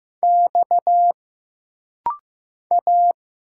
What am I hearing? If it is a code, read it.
Morse code, heard as a keyed tone, about 15 wpm, 700 Hz. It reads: X A
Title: Single Letters - Rapid-Fire 15wpm